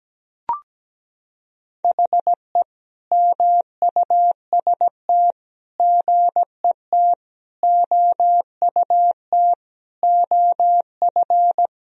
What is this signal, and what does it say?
Morse code, heard as a keyed tone, about 17 wpm, 700 Hz. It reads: HE MUST GET OUT OF
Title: Sentences from Top 1000 Words - Review Part 1 - 17wpm